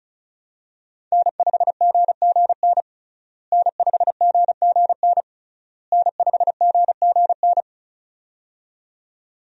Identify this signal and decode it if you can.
Morse code, heard as a keyed tone, about 36 wpm, 700 Hz. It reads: N5GGN N5GGN N5GGN
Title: USA Call Signs - 5 Characters - Repeated 3x - 35wpm